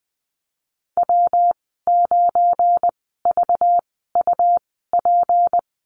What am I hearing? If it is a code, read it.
Morse code, heard as a keyed tone, about 19 wpm, spaced as for 14 wpm, 700 Hz. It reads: W9VUP
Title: 193 Lesson 31g - Letter v - Call Signs 20wpm